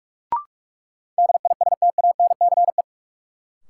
Morse code, heard as a keyed tone, about 42 wpm, 700 Hz. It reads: DISTANCE
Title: Sentences from Top 1200 Words - New Word Drill Alt - 45wpm